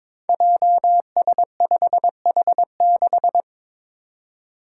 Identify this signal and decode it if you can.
Morse code, heard as a keyed tone, about 22 wpm, 700 Hz. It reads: JS5H6
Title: Five Challenging Letter-Number Combinations 22wpm